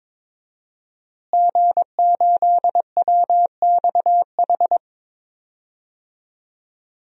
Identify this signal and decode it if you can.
Morse code, heard as a keyed tone, about 22 wpm, 700 Hz. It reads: G8WXH
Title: DXCC-100 Call Signs - 5 Characters - Repeated 2x - 22wpm